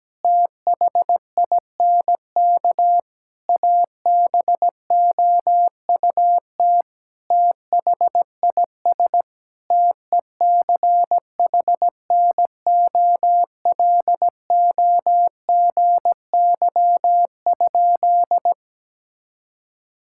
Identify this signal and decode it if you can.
Morse code, heard as a keyed tone, about 17 wpm, 700 Hz. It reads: THINK ABOUT THIS TECHNOLOGY?
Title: Sentences from Top 1000 Words - Review Part 5 - 17wpm